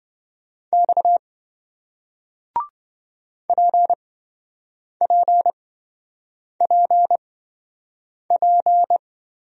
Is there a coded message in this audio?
Yes: Morse code from 0.7 to 9.0 s, heard as a keyed tone, about 28 wpm, spaced as for 7 wpm, 700 Hz. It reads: X PPPP